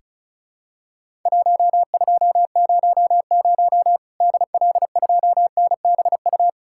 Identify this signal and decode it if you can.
Morse code, heard as a keyed tone, about 36 wpm, 700 Hz. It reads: 1200 DL2NBU
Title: Worked All Europe DX Contest - QTC Exchange - 35wpm